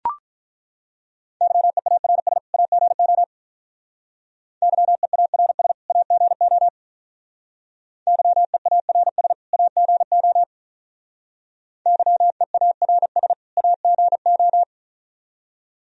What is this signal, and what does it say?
Morse code, heard as a keyed tone, about 43 wpm, 700 Hz. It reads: YEARS AGO YEARS AGO YEARS AGO YEARS AGO
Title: Speed-Racing - Sets of 2 Words 35wpm